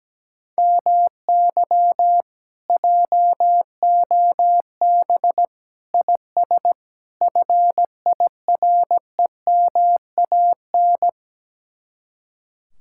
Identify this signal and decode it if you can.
Morse code, heard as a keyed tone, about 17 wpm, 700 Hz. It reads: MY JOB IS FIREMAN
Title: Speed-Racing Common QSO Phrases - 17wpm